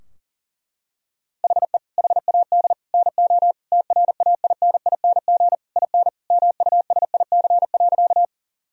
Morse code, heard as a keyed tone, about 40 wpm, 700 Hz. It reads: HE HAD NO TRAINING IN MUSIC.
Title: Sentences from Top 1000 Words - Review Part 4 - 40wpm